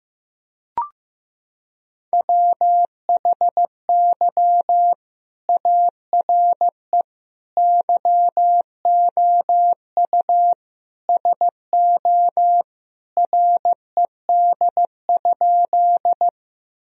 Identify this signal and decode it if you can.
Morse code, heard as a keyed tone, about 15 wpm, 700 Hz. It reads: WHY ARE YOU SO RED?
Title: Sentences from Top 1000 Words - Review Part 2 - 15wpm